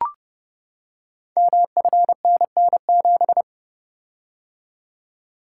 Morse code, heard as a keyed tone, about 28 wpm, 700 Hz. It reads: MFNN7
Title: Code Groups - 5 Letter:Numbers - 30wpm